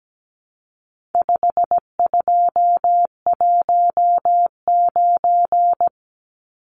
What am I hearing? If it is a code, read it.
Morse code, heard as a keyed tone, about 17 wpm, 700 Hz. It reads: 5219
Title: Numbers - 4-Digit 17wpm